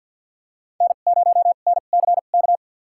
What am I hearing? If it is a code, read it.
Morse code, heard as a keyed tone, about 52 wpm, spaced as for 36 wpm, 700 Hz. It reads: N0NCK
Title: USA Call Signs - 5 Characters - 2x Character Spacing - 50wpm